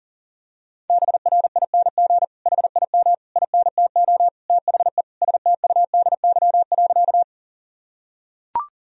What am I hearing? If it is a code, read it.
Morse code, heard as a keyed tone, about 40 wpm, 700 Hz. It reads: BRING HIM INTO THE STUDY.
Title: Sentences from Top 1000 Words - Review Part 1 - 40wpm